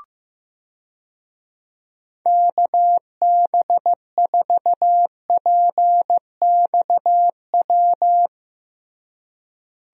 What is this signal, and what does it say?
Morse code, heard as a keyed tone, about 15 wpm, 700 Hz. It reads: KB4PXW
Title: USA Call Signs - 6 Characters - Repeated 2x - 15wpm